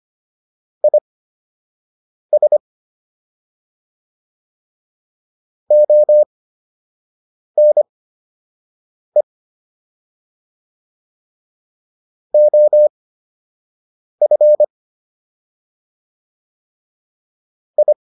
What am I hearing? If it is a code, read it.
Morse code, heard as a keyed tone, about 25 wpm, spaced as for 6 wpm, 600 Hz. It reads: IS ONE OF I